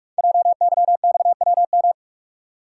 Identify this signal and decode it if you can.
Morse code, heard as a keyed tone, about 44 wpm, 690 Hz. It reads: JYXWM
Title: Code Groups - 5 Letters - 45wpm